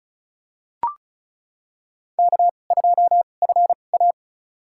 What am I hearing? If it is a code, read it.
Morse code, heard as a keyed tone, about 36 wpm, spaced as for 25 wpm, 690 Hz. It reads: K2FA
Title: USA Call Signs - 4 Characters - 2x Character Spacing - 35wpm